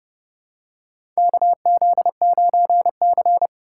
Morse code, heard as a keyed tone, about 30 wpm, 700 Hz. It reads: KZ9C
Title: USA Call Signs - 4 Characters - 4x Character Spacing - 30wpm